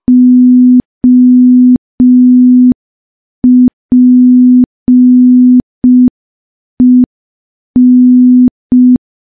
Morse code, heard as a keyed tone, about 5 wpm, 250 Hz. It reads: OPEN
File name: morse / morse-training-05WPM